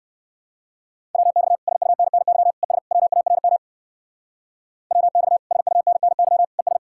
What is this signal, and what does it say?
Morse code, heard as a keyed tone, about 54 wpm, 700 Hz. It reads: WX SUNNY ES WARM WX SUNNY ES
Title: Speed-Racing Common QSO Phrases - 45wpm